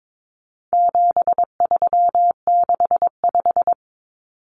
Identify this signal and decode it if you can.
Morse code, heard as a keyed tone, about 22 wpm, 700 Hz. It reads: 7365
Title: Common Cut Numbers - 4-Digit 22wpm